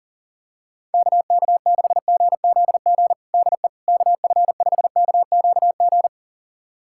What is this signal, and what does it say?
Morse code, heard as a keyed tone, about 40 wpm, 700 Hz. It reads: KK6GZG DE KF5KQG
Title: USA Callsign DE Callsign - 6 Characters - 40wpm